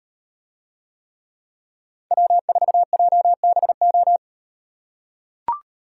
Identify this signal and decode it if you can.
Morse code, heard as a keyed tone, about 38 wpm, 700 Hz. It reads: W4JBO